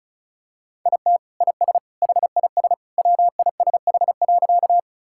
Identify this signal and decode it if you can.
Morse code, heard as a keyed tone, about 34 wpm, 700 Hz. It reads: IT IS HIS WISH.
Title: Sentences from Top 1600 Words 35wpm